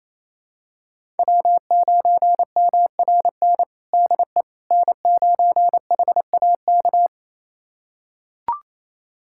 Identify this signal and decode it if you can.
Morse code, heard as a keyed tone, about 28 wpm, 700 Hz. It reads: W9MRN DE N9HAK
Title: USA Callsign DE Callsign - 5 Characters - 28wpm